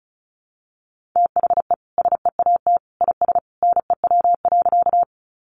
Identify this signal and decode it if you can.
Morse code, heard as a keyed tone, about 34 wpm, 700 Hz. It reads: THE SEAT IS NEW.
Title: Sentences from Top 1500 Words 35wpm